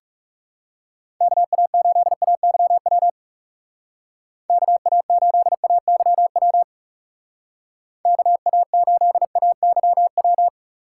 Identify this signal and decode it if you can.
Morse code, heard as a keyed tone, about 37 wpm, 700 Hz. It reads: KA8AYW KA8AYW KA8AYW
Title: Speed Racing - USA 2x3 Call Signs 30wpm